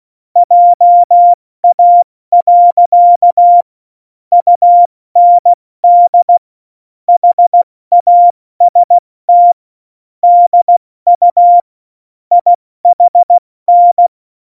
Morse code, heard as a keyed tone, about 16 wpm, 700 Hz. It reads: JA. UND HAST DU IHN
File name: Morsekurs / Witze / Witz 19. September 17, Tempo 70